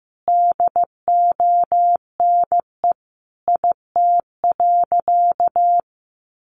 Morse code, heard as a keyed tone, about 15 wpm, 700 Hz. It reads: DONE IT.
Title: Sentences from Top 1200 Words 15wpm